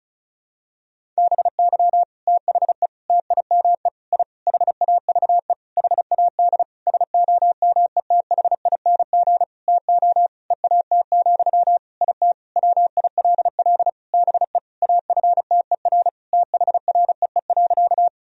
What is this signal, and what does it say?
Morse code, heard as a keyed tone, about 34 wpm, 700 Hz. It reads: BY THE TIME I HAVE HAD SOMETHING TO EAT, IT WILL BE AFTER THREE.
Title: Sentences from Top 1000 Words - Review Part 1 - 35wpm